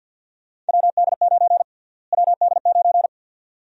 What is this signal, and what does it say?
Morse code, heard as a keyed tone, about 48 wpm, 700 Hz. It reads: WD9 WD9
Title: USA Call Sign Prefixes - Repeated 2x - 50wpm